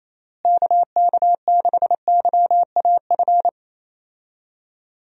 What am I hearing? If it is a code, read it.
Morse code, heard as a keyed tone, about 28 wpm, 700 Hz. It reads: KK6YAF
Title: USA Call Signs - 6 Characters - Repeated 3x - 28wpm